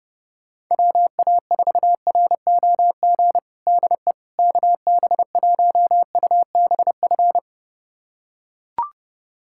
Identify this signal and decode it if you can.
Morse code, heard as a keyed tone, about 30 wpm, 700 Hz. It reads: WA4ROG DE KB1UBF
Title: USA Callsign DE Callsign - 6 Characters - 30wpm